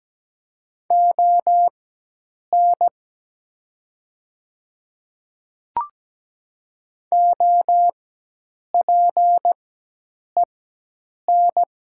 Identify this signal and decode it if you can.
Morse code, heard as a keyed tone, about 17 wpm, spaced as for 8 wpm, 700 Hz. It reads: ON OPEN
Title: 4x Character Spacing - Top 500 Words - 17wpm